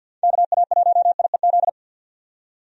Morse code, heard as a keyed tone, about 51 wpm, 700 Hz. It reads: KA1IEZ
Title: USA Call Signs - 6 Characters - 50wpm